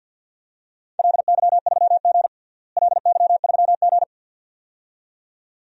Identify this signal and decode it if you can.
Morse code, heard as a keyed tone, about 51 wpm, 700 Hz. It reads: LY3G LY3G
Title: DXCC-100 Call Signs - 4 Characters - Repeated 2x - 50wpm